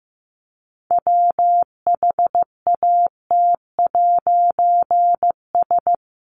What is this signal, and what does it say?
Morse code, heard as a keyed tone, about 15 wpm, 700 Hz. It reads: WHAT'S
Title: A Sound Of Thunder 15wpm